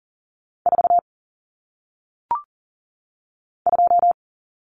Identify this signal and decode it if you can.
Morse code, heard as a keyed tone, about 39 wpm, 700 Hz. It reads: 4 2
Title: Numbers - 1-Digit 40wpm